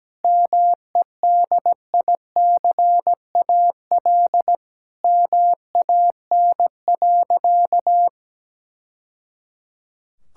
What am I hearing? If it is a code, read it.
Morse code, heard as a keyed tone, about 17 wpm, 700 Hz. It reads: MEDICAL MAN.
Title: Sentences from Top 1000 Words - Review Part 5 - 17wpm